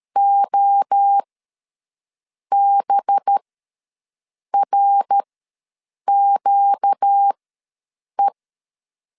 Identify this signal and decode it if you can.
Morse code, heard as a keyed tone, about 13 wpm, spaced as for 6 wpm, 800 Hz. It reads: OBRQE
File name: 5 LB ws 12wpm